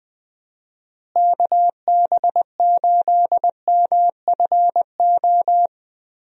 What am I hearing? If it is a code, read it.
Morse code, heard as a keyed tone, about 20 wpm, 700 Hz. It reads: KB8MFO
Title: USA Call Signs - 6 Characters - Repeated 2x - 20wpm